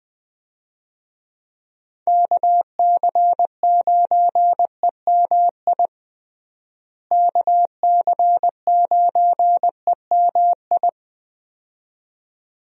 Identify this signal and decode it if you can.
Morse code, heard as a keyed tone, about 21 wpm, 700 Hz. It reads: KC9EMI KC9EMI
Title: USA Call Signs - 6 Characters - Repeated 2x - 20wpm